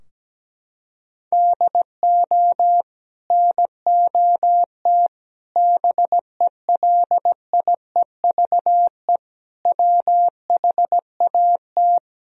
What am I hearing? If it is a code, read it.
Morse code, heard as a keyed tone, about 17 wpm, 700 Hz. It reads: DO NOT BELIEVE WHAT
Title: Sentences from Top 1000 Words - Review Part 4 - 17wpm